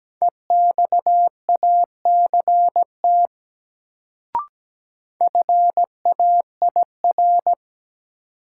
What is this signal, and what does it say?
Morse code, heard as a keyed tone, about 17 wpm, 700 Hz. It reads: EXACT FAIR
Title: Sentences from Top 1300 Words - New Word Drill Alt - 17wpm